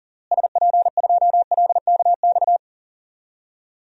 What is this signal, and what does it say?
Morse code, heard as a keyed tone, about 40 wpm, 700 Hz. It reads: SP2LKX
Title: DXCC-100 Call Signs - 6 Characters - Repeated 3x - 40wpm